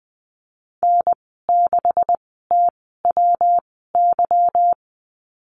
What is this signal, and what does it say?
Morse code, heard as a keyed tone, about 20 wpm, spaced as for 15 wpm, 700 Hz. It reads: N6TWY
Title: 147 Lesson 24g - Letter y - Call Signs 20wpm